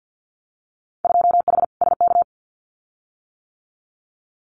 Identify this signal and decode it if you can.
Morse code, heard as a keyed tone, about 50 wpm, 700 Hz. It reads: 2H SK
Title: Winter Field Day Exchanges Encore 50wpm